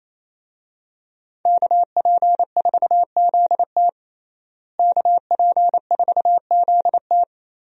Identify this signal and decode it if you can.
Morse code, heard as a keyed tone, about 28 wpm, 700 Hz. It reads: KP4ZT KP4ZT
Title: USA Call Signs - 5 Characters - Repeated 2x - 28wpm